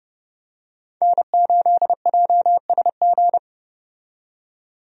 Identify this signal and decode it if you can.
Morse code, heard as a keyed tone, about 32 wpm, 700 Hz. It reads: N8JSG
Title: USA Call Signs - 5 Characters - 30wpm